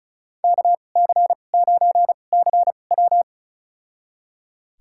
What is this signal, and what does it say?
Morse code, heard as a keyed tone, about 34 wpm, spaced as for 25 wpm, 700 Hz. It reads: KC9CW
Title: USA Call Signs - 2x Character Spacing - 35wpm